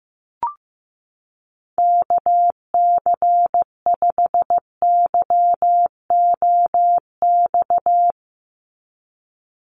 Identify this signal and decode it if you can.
Morse code, heard as a keyed tone, about 15 wpm, 700 Hz. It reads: KC5YOX
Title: USA Call Signs 15wpm